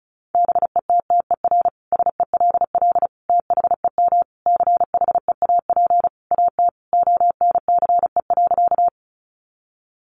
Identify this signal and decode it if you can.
Morse code, heard as a keyed tone, about 34 wpm, 700 Hz. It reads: BETTER SELL THEM CHEAP AT ONCE.
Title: Sentences from Top 1200 Words 35wpm